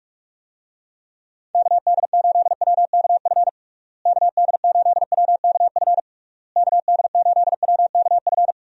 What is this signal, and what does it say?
Morse code, heard as a keyed tone, about 46 wpm, 700 Hz. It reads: KD8WKF KD8WKF KD8WKF
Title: USA Call Signs - 6 Characters - Repeated 3x - 45wpm